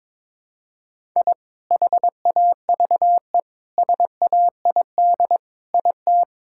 Morse code, heard as a keyed tone, about 22 wpm, 700 Hz. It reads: I HAVE SAID IT
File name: practice / 22wpm / Sentences with 8-Word Phrases - 22wpm